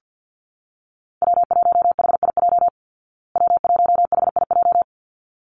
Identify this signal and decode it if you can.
Morse code, heard as a keyed tone, about 50 wpm, 700 Hz. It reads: W1HIJ W1HIJ
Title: Call Signs Repeated 3x - 50wpm